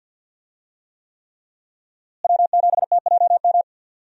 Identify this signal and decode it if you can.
Morse code, heard as a keyed tone, about 51 wpm, 700 Hz. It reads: W7TJM
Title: USA Call Signs - 5 Characters - Repeated 3x - 50wpm